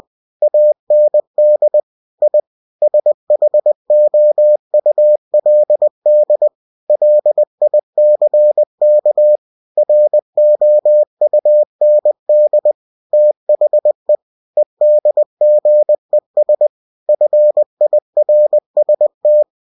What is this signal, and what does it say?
Morse code, heard as a keyed tone, about 20 wpm, 600 Hz. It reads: AND I SHOULD LICK ROUND THE EDGES FIRST
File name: cw-books / winnie-the-pooh / winnie-20wpm-page-0036